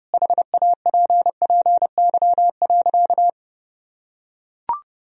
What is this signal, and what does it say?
Morse code, heard as a keyed tone, about 30 wpm, 700 Hz. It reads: HAPPY.
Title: Sentences from Top 1000 Words - Review Part 3 - 30wpm